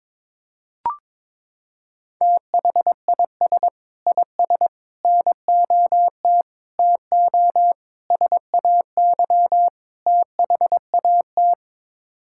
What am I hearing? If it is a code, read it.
Morse code, heard as a keyed tone, about 22 wpm, 700 Hz. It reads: THIS IS NOT TO SAY THAT
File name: practice / 22wpm / Sets of 6 Words - 22wpm